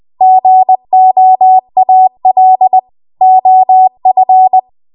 Morse code, heard as a keyed tone, about 20 wpm, 750 Hz. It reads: GOAL OF
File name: Morse / 20WPM